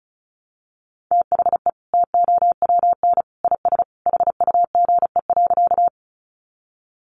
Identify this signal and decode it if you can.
Morse code, heard as a keyed tone, about 34 wpm, 700 Hz. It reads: THE TOWN IS HUGE.
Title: Sentences from Top 1100 Words 35wpm